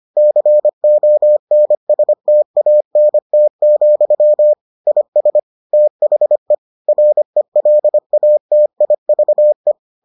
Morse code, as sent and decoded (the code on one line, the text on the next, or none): -.-. --- -. ... - .- -. - --..-- / .. ... / - .... / .-. .-.. .- - .. ...- .
CONSTANT, IS THE RELATIVE